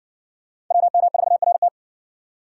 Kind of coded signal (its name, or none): Morse code